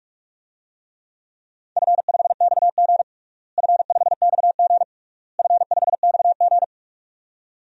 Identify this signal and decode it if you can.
Morse code, heard as a keyed tone, about 42 wpm, 700 Hz. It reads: F5XG F5XG F5XG
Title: DXCC-100 Call Signs - 4 Characters - Repeated 3x - 45wpm